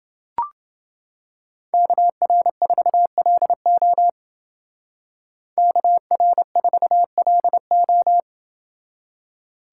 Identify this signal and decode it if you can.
Morse code, heard as a keyed tone, about 28 wpm, 700 Hz. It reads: KR4LO KR4LO